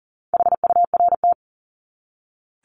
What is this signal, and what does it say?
Morse code, heard as a keyed tone, about 40 wpm, 700 Hz. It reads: HURT
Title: Top 1751-2000 Words 40wpm